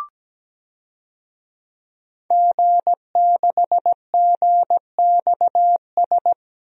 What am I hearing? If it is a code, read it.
Morse code, heard as a keyed tone, about 17 wpm, 700 Hz. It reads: G6GXS
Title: DXCC-100 Call Signs - 5 Characters - Repeated 2x - 17wpm